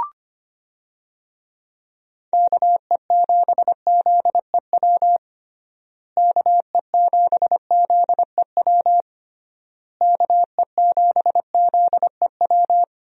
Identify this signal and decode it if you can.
Morse code, heard as a keyed tone, about 25 wpm, 700 Hz. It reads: KE7ZEW KE7ZEW KE7ZEW
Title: USA Call Signs - 6 Characters - Repeated 3x - 25wpm